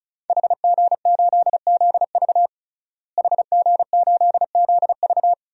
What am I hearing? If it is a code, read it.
Morse code, heard as a keyed tone, about 34 wpm, 690 Hz. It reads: HG8ZV HG8ZV
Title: DXCC-100 Call Signs - 5 Characters - Repeated 3x - 35wpm